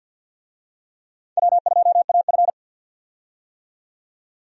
Morse code, heard as a keyed tone, about 49 wpm, 700 Hz. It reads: W2AF
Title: USA Call Signs - 4 Characters - 50wpm